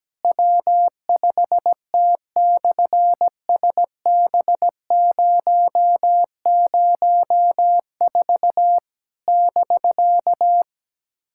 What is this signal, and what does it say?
Morse code, heard as a keyed tone, about 17 wpm, 700 Hz. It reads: W5T/SB004 <BK>
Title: SOTA QSO Practice 17wpm